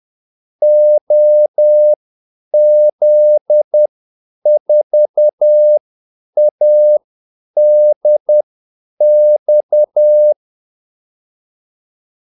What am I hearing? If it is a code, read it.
Morse code, heard as a keyed tone, about 10 wpm, spaced as for 8 wpm, 600 Hz. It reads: OZ4ADX
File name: Callsigns-10-8wpm